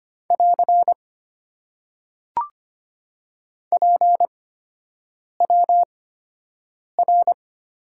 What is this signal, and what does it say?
Morse code, heard as a keyed tone, about 25 wpm, spaced as for 7 wpm, 700 Hz. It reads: <AR> PWR
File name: Top 90 QSO Elements - 8x Character Spacing - 25wpm